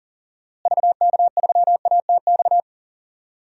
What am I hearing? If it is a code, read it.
Morse code, heard as a keyed tone, about 37 wpm, 700 Hz. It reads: VK3ATX
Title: DXCC-100 Call Signs - 6 Characters - Repeated 3x - 40wpm